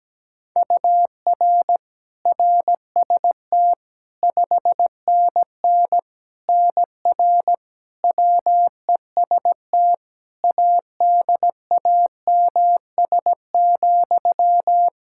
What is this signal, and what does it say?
Morse code, heard as a keyed tone, about 17 wpm, 700 Hz. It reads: UR RST 5NN NR WEST ADAMS,
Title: Practice QSO - Exchange 1 - Beginning - 17wpm